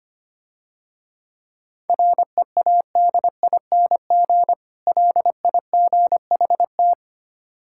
Morse code, heard as a keyed tone, about 26 wpm, 700 Hz. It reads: READING LIGHT
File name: practice / 25wpm / Difficult Words - Speed-Racing with Hints - 25wpm